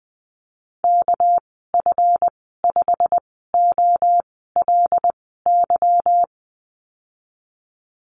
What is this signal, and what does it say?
Morse code, heard as a keyed tone, about 20 wpm, spaced as for 15 wpm, 700 Hz. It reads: KF5OLY
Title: 213 Lesson 34g - Number 8 - Call Signs 20wpm